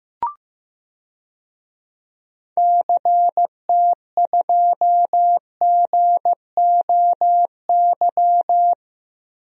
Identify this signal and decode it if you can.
Morse code, heard as a keyed tone, about 15 wpm, 700 Hz. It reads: CT2GOY
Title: DXCC-100 Call Signs - 6 Characters - Repeated 2x - 15wpm